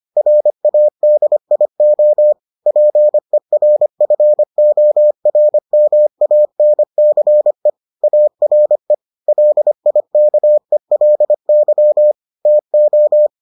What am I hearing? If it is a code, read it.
Morse code, heard as a keyed tone, about 25 wpm, 600 Hz. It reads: RADIO PERFORMANCE ARE LIKELY TO